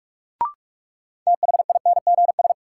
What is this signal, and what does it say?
Morse code, heard as a keyed tone, about 46 wpm, 700 Hz. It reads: THINGS
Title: Sentences from Top 1000 Words - Review Part 4 - New Word Drill - 45wpm